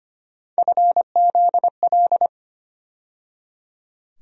Morse code, heard as a keyed tone, about 25 wpm, 700 Hz. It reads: FZL
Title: USA Call Sign Postfixes - 3 Letters 25wpm